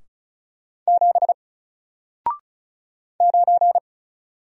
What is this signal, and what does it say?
Morse code, heard as a keyed tone, about 33 wpm, 700 Hz. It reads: 7 9
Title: Single Letter-Number Alt 35wpm